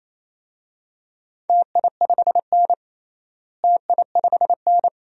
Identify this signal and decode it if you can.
Morse code, heard as a keyed tone, about 28 wpm, 700 Hz. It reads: TI5N TI5N
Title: DXCC-100 Call Signs - 4 Characters - Repeated 2x - 28wpm